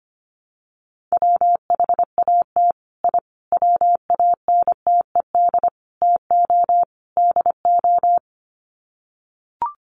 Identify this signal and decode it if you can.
Morse code, heard as a keyed tone, about 25 wpm, 700 Hz. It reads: WHAT I WANTED TO DO
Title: Sets of 5 Words Encore2 25wpm